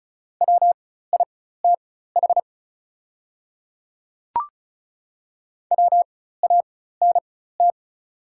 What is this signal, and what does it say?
Morse code, heard as a keyed tone, about 36 wpm, spaced as for 16 wpm, 700 Hz. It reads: WITH WANT
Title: 4x Character Spacing - Top 200 Words - 35wpm